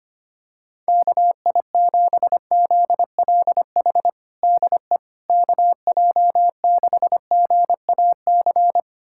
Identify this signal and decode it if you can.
Morse code, heard as a keyed tone, about 25 wpm, 700 Hz. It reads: KI7ZLH DE KJ6GAC
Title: USA Callsign DE Callsign - 6 Characters - 25wpm